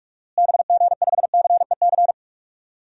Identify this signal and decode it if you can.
Morse code, heard as a keyed tone, about 46 wpm, 700 Hz. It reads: BG5CEC
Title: DXCC-100 Call Signs - 6 Characters - Repeated 3x - 45wpm